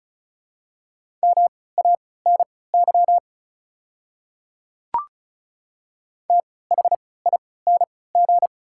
Morse code, heard as a keyed tone, about 36 wpm, spaced as for 20 wpm, 700 Hz. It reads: MANY THING
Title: 3x Character Spacing - Top 200 Words - 35wpm